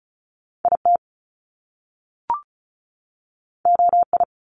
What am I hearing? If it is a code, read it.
Morse code, heard as a keyed tone, about 36 wpm, 700 Hz. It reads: IT OI
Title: Top Two Letter Combinations in Words 35wpm